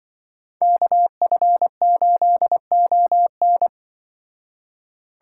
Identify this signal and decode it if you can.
Morse code, heard as a keyed tone, about 25 wpm, 700 Hz. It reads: KF8ON
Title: Speed Racing - USA 2x2 Call Signs 20wpm